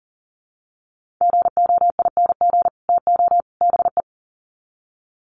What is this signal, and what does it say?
Morse code, heard as a keyed tone, about 40 wpm, 700 Hz. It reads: GOING TO BE
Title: Sets of 3 Words 40wpm